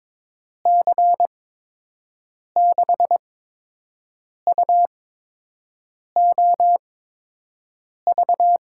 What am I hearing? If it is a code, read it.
Morse code, heard as a keyed tone, about 22 wpm, spaced as for 6 wpm, 700 Hz. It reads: C6UOV
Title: USA Call Signs - 8x Character Spacing - 22wpm